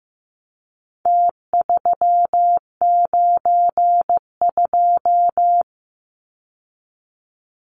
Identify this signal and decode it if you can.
Morse code, heard as a keyed tone, about 15 wpm, 700 Hz. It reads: T392
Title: Numbers - Contest Style - 4-Digit 15wpm